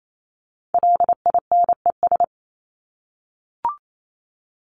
Morse code, heard as a keyed tone, about 28 wpm, 700 Hz. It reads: LINES
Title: ES:S Suffix - Top 100 Words with Base Words 28wpm